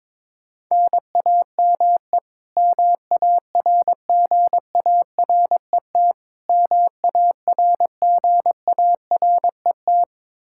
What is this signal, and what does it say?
Morse code, heard as a keyed tone, about 22 wpm, 700 Hz. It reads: NAME MARGARET MARGARET
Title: Practice QSO - Exchange 1 - Beginning - 22wpm